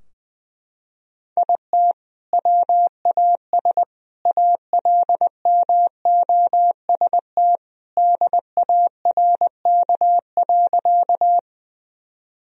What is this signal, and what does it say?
Morse code, heard as a keyed tone, about 20 wpm, 700 Hz. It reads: IT WAS ALMOST DARK.
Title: Sentences from Top 1000 Words - Review Part 4 - 20wpm